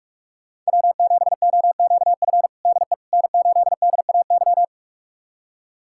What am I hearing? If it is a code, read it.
Morse code, heard as a keyed tone, about 46 wpm, 700 Hz. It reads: W7OQF DE N8DAY
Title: USA Callsign DE Callsign - 5 Characters - 45wpm